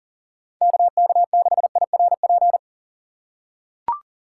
Morse code, heard as a keyed tone, about 40 wpm, 700 Hz. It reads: KK6IRP
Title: USA Call Signs - 6 Characters - 3x Character Spacing - 40wpm